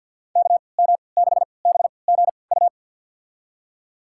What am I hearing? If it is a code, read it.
Morse code, heard as a keyed tone, about 52 wpm, spaced as for 29 wpm, 700 Hz. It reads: KM6BGU
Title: USA Call Signs - 6 Characters - 3x Character Spacing - 50wpm